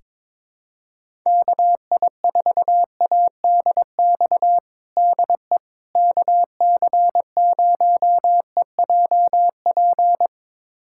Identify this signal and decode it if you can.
Morse code, heard as a keyed tone, about 22 wpm, 700 Hz. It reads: KI4ADX DE KC0EJP
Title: USA Callsign DE Callsign - 6 Characters - 22wpm